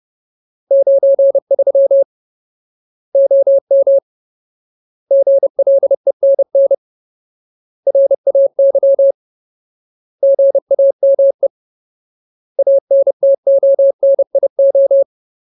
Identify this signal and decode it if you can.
Morse code, heard as a keyed tone, about 30 wpm, 550 Hz. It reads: 93 OM GLENN RAY GAME ANTONIO
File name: Ultimate Word List / 30WPM Ultimate